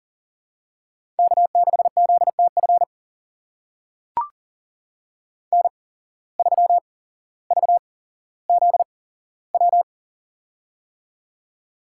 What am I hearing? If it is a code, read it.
Morse code, heard as a keyed tone, about 40 wpm, spaced as for 17 wpm, 700 Hz. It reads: K6ZTF N3VZW
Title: USA Call Signs - 5 Characters - 8x Character Spacing - 40wpm